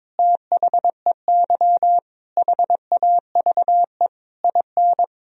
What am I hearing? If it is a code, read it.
Morse code, heard as a keyed tone, about 22 wpm, 700 Hz. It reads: THEY HAVE IN
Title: Sentences from Top 1000 Words - Review Part 3 - 22wpm